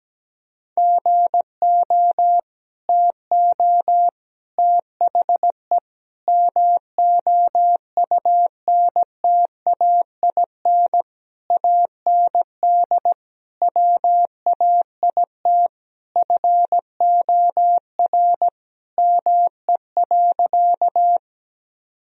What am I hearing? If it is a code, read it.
Morse code, heard as a keyed tone, about 17 wpm, 700 Hz. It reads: GO TO THE MOUNTAIN AND WAIT FOR ME.
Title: Sentences from Top 1000 Words - Review Part 1 - 17wpm